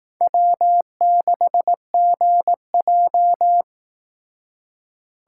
Morse code, heard as a keyed tone, about 18 wpm, 700 Hz. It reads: W6GJ
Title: Speed Racing - USA 1x2 Call Signs 15wpm